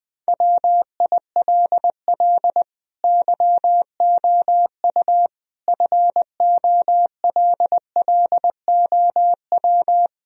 Morse code, heard as a keyed tone, about 20 wpm, 700 Hz. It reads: WILL YOU FOLLOW
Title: Sentences from Top 1000 Words - Review Part 1 - 20wpm